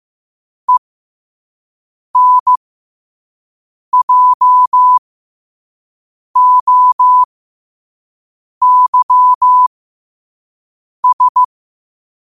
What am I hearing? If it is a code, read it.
Morse code, heard as a keyed tone, about 15 wpm, spaced as for 5 wpm, 1000 Hz. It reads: ENJOYS
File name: qotdfiles / 5wpm